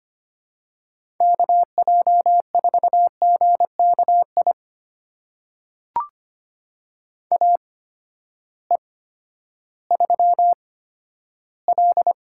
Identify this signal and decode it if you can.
Morse code, heard as a keyed tone, about 25 wpm, spaced as for 12 wpm, 700 Hz. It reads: KJ4GKI AE3L
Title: USA Call Signs - 8x Character Spacing - 25wpm